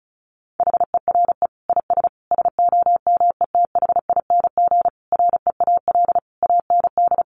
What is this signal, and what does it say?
Morse code, heard as a keyed tone, about 34 wpm, 700 Hz. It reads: HERE IS SOMETHING REAL AND
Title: Sentences from Top 1500 Words 35wpm